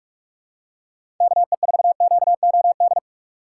Morse code, heard as a keyed tone, about 41 wpm, 700 Hz. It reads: KE4QOD
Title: USA Call Signs - 6 Characters - 2x Character Spacing - 45wpm